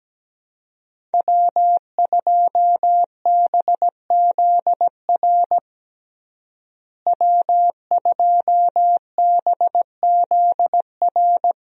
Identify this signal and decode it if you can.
Morse code, heard as a keyed tone, about 17 wpm, 700 Hz. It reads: W2BZR W2BZR
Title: USA Call Signs - 5 Characters - Repeated 3x - 17wpm